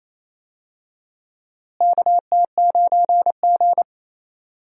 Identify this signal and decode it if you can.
Morse code, heard as a keyed tone, about 27 wpm, 700 Hz. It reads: KT9G
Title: USA Call Signs - 4 Characters - Repeated 2x - 28wpm